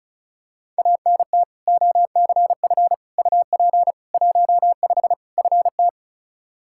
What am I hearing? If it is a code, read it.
Morse code, heard as a keyed tone, about 36 wpm, 700 Hz. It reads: ANT OCF UP 15 FT
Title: Practice QSO - Exchange 2 - Info - 35wpm